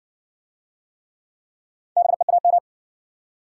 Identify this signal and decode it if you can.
Morse code, heard as a keyed tone, about 56 wpm, 700 Hz. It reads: BEAM